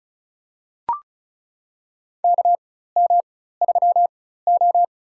Morse code, heard as a keyed tone, about 36 wpm, spaced as for 16 wpm, 700 Hz. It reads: KM3O
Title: USA Call Signs - 4 Characters - 4x Character Spacing - 35wpm